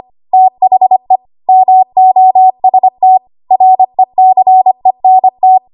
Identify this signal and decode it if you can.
Morse code, heard as a keyed tone, about 25 wpm, 750 Hz. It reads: THE MOST RECENT
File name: Archive / 25 WPM / 25WPM